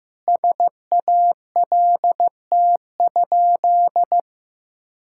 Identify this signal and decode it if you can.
Morse code, heard as a keyed tone, about 15 wpm, 700 Hz. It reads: SALT?
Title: Sentences from Top 1000 Words - Review Part 5 - 15wpm